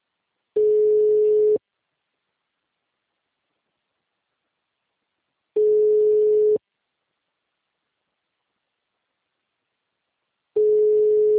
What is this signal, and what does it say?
Ringback tone (Europe, 425 Hz).